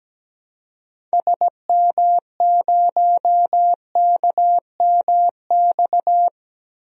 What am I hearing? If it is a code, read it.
Morse code, heard as a keyed tone, about 17 wpm, 700 Hz. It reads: SM0KMX
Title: DXCC-100 Call Signs - 6 Characters - Repeated 2x - 17wpm